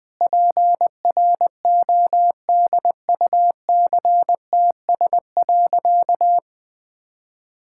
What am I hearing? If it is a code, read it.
Morse code, heard as a keyed tone, about 21 wpm, 700 Hz. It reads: PRODUCTS.
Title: Sentences with 6-Word Phrases - Spoken Beginning - 20wpm